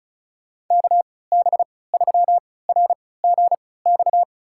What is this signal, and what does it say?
Morse code, heard as a keyed tone, about 36 wpm, spaced as for 20 wpm, 700 Hz. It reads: KB3RGX
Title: USA Call Signs - 6 Characters - 3x Character Spacing - 35wpm